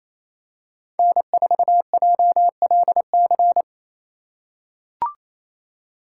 Morse code, heard as a keyed tone, about 28 wpm, 700 Hz. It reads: N4JLC
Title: USA Call Signs - 5 Characters - 4x Character Spacing - 28wpm